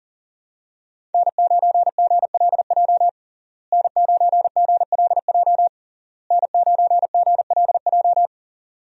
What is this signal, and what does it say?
Morse code, heard as a keyed tone, about 40 wpm, 700 Hz. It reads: N9GLJ N9GLJ N9GLJ
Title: USA Call Signs - 5 Characters - Repeated 3x - 40wpm